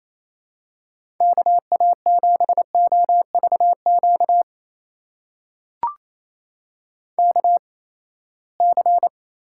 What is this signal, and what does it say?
Morse code, heard as a keyed tone, about 28 wpm, spaced as for 19 wpm, 700 Hz. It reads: KA7OVQ KC
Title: USA Call Signs - 6 Characters - 8x Character Spacing - 28wpm